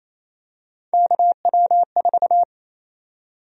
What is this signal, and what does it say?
Morse code, heard as a keyed tone, about 28 wpm, 700 Hz. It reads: KW4
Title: USA Call Sign Prefixes - 28wpm